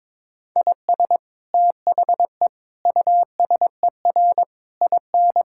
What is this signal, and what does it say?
Morse code, heard as a keyed tone, about 22 wpm, 700 Hz. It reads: IS THE USER IN
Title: Sentences from Top 1000 Words - Review Part 5 - 22wpm